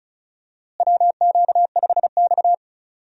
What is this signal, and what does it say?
Morse code, heard as a keyed tone, about 36 wpm, 700 Hz. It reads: WQ5X
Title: USA Call Signs - 4 Characters - Repeated 2x - 35wpm